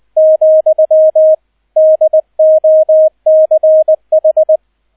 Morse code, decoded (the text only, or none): , DOCH